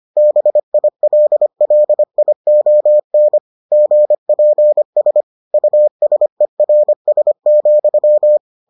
Morse code, heard as a keyed tone, about 25 wpm, 600 Hz. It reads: BILLION GPS USERS,